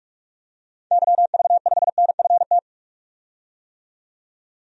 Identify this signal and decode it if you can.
Morse code, heard as a keyed tone, about 47 wpm, 700 Hz. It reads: YV5NFT
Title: DXCC-100 Call Signs - 6 Characters - 45wpm